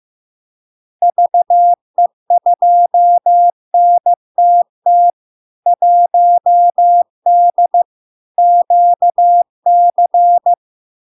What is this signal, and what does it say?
Morse code, heard as a keyed tone, about 15 wpm, 700 Hz. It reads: VE2NTT 1D QC
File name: LARK-FD-2022-Log-in-CW-15wpm-700-Hz